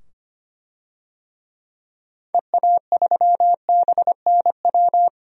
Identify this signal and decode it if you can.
Morse code, heard as a keyed tone, about 25 wpm, 700 Hz. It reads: EA3BNW
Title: DXCC-100 Call Signs - 6 Characters - 25wpm